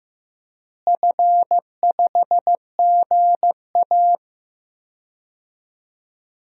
Morse code, heard as a keyed tone, about 15 wpm, 700 Hz. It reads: F5GA
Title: DXCC-100 Call Signs - 4 Characters - Repeated 3x - 15wpm